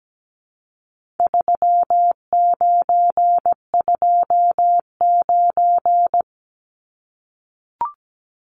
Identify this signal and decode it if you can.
Morse code, heard as a keyed tone, about 17 wpm, 700 Hz. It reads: 3929
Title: Numbers - 4-Digit 17wpm